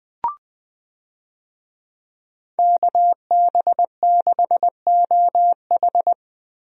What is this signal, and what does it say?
Morse code, heard as a keyed tone, about 20 wpm, 700 Hz. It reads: KB6OH
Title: USA Call Signs - 5 Characters - Repeated 2x - 20wpm